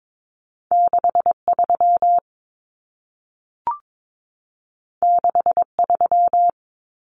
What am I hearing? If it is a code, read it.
Morse code, heard as a keyed tone, about 22 wpm, 700 Hz. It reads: 63 63
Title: Numbers - 2-Digit 22wpm